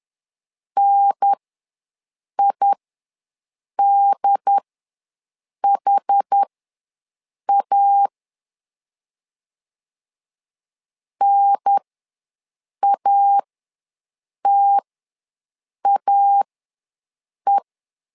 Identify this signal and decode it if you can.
Morse code, heard as a keyed tone, about 10 wpm, spaced as for 6 wpm, 800 Hz. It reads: NIDHA NATAE
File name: 2 UHDN ws 10wpm